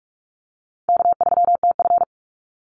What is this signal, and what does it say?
Morse code, heard as a keyed tone, about 44 wpm, 700 Hz. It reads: K3TF
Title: Easy Call Signs - 45wpm